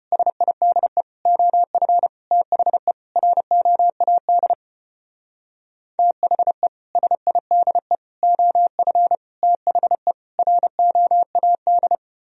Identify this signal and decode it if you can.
Morse code, heard as a keyed tone, about 32 wpm, 700 Hz. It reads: SIDE OF THE ROAD THE SIDE OF THE ROAD